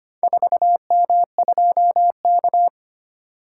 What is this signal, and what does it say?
Morse code, heard as a keyed tone, about 25 wpm, 700 Hz. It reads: 4M2K
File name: practice / 25wpm / Four Letter-Number Combinations - 25wpm